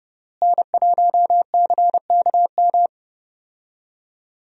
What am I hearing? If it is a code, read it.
Morse code, heard as a keyed tone, about 30 wpm, 700 Hz. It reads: N1CKM
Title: USA Call Signs - 5 Characters - Repeated 3x - 30wpm